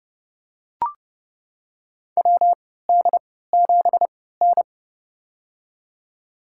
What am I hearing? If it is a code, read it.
Morse code, heard as a keyed tone, about 30 wpm, spaced as for 17 wpm, 700 Hz. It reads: WD7N
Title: USA Call Signs - 4 Characters - 3x Character Spacing - 30wpm